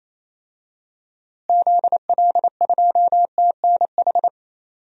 Morse code, heard as a keyed tone, about 28 wpm, 700 Hz. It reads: ZL2TNH
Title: DXCC-100 Call Signs - 6 Characters - Repeated 2x - 28wpm